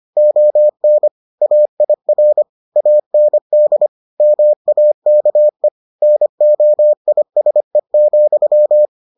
Morse code, heard as a keyed tone, about 25 wpm, 600 Hz. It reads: ON AIR AND MAKE NOISE,